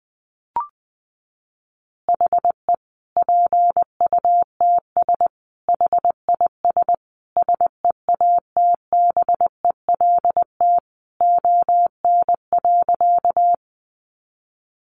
Morse code, heard as a keyed tone, about 20 wpm, 700 Hz. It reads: HE PUTS HIS SEATBELT ON.
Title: Additional Easy Phrases 20wpm